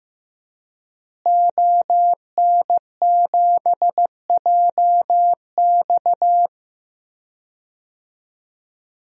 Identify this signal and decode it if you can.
Morse code, heard as a keyed tone, about 15 wpm, 700 Hz. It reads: ON7JX
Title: DXCC-100 Call Signs - 5 Characters - Repeated 3x - 15wpm